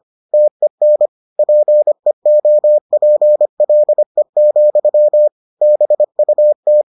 Morse code, heard as a keyed tone, about 25 wpm, 600 Hz. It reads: TEN PEOPLE, BUT